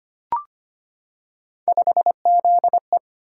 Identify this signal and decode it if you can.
Morse code, heard as a keyed tone, about 25 wpm, 700 Hz. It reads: 5ZE
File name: Three Letter-Number Combinations - 25wpm